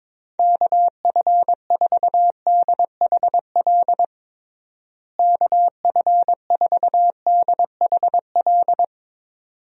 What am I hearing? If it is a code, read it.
Morse code, heard as a keyed tone, about 23 wpm, 700 Hz. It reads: KF4DHL KF4DHL
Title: USA Call Signs - 6 Characters - Repeated 2x - 22wpm